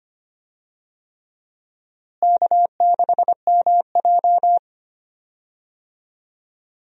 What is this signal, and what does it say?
Morse code, heard as a keyed tone, about 25 wpm, 700 Hz. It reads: K6MJ
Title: USA Call Signs - 4 Characters - Repeated 3x - 25wpm